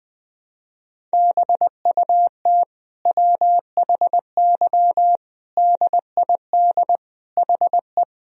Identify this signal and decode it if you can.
Morse code, heard as a keyed tone, about 20 wpm, 700 Hz. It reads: BUT WHY DID HE
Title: Sentences from Top 1000 Words - Review Part 2 - 20wpm